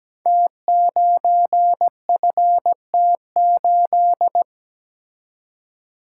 Morse code, heard as a keyed tone, about 17 wpm, 700 Hz. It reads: T9FT8
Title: Code Groups - 5 Letter:Numbers - 17wpm